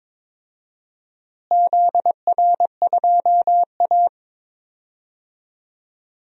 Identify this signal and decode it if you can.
Morse code, heard as a keyed tone, about 22 wpm, 700 Hz. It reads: ZR2A
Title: DXCC-100 Call Signs - 4 Characters - Repeated 2x - 22wpm